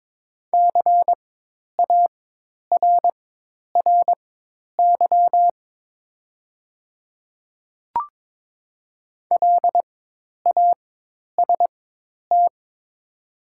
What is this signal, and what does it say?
Morse code, heard as a keyed tone, about 22 wpm, spaced as for 10 wpm, 700 Hz. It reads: CARRY LAST
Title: 4x Character Spacing - Top 300 Words - 22wpm